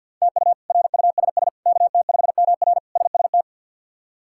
Morse code, heard as a keyed tone, about 48 wpm, 700 Hz. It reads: TU RUSS KT5MR SST
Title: K1USN SST QSO Practice - 50wpm